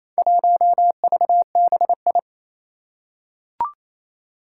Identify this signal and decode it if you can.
Morse code, heard as a keyed tone, about 28 wpm, 700 Hz. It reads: 1VBI
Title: Four Challenging Letter-Number Combinations 28wpm